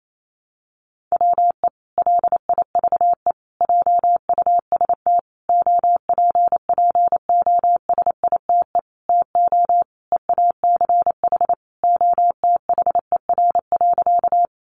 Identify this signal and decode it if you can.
Morse code, heard as a keyed tone, about 28 wpm, 700 Hz. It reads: WE LIVE JUST OPPOSITE TO EACH OTHER.
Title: Sentences from Top 1300 Words 28wpm